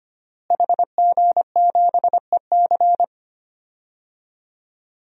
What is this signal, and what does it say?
Morse code, heard as a keyed tone, about 25 wpm, 700 Hz. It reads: HG7EC
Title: DXCC-100 Call Signs - 5 Characters - Repeated 3x - 25wpm